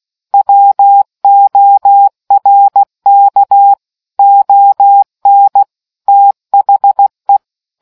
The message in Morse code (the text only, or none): WORK ON THE